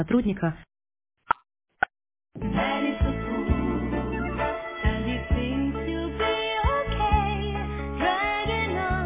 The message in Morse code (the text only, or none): none